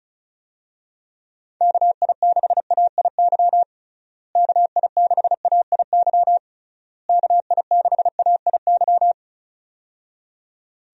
Morse code, heard as a keyed tone, about 34 wpm, 700 Hz. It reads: KI6AIY KI6AIY KI6AIY
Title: USA Call Signs - 6 Characters - Repeated 3x - 35wpm